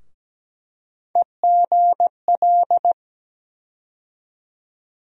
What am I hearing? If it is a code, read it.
Morse code, heard as a keyed tone, about 17 wpm, 700 Hz. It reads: EGL
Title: USA Call Sign Postfixes - 3 Letters 17wpm